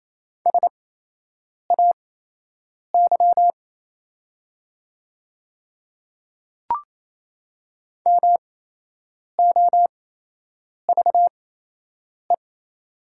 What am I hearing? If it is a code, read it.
Morse code, heard as a keyed tone, about 28 wpm, spaced as for 8 wpm, 700 Hz. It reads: SAY MOVE
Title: 8x Character Spacing - Top 200 Words - 28wpm